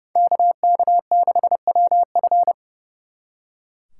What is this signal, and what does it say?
Morse code, heard as a keyed tone, about 39 wpm, 700 Hz. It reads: KK6WF